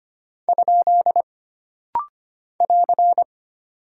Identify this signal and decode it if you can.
Morse code, heard as a keyed tone, about 25 wpm, 700 Hz. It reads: ? <AR>
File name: Single Punctuation-Prosign - Rapid-Fire - 25wpm